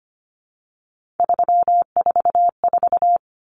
Morse code, heard as a keyed tone, about 25 wpm, 700 Hz. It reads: 344
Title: Numbers - 3-Digit 25wpm